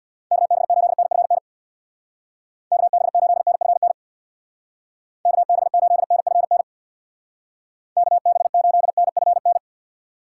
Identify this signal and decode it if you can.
Morse code, heard as a keyed tone, about 62 wpm, 700 Hz. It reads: KB8NFN KB8NFN KB8NFN KB8NFN
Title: Speed Racing - USA 2x3 Call Signs 50wpm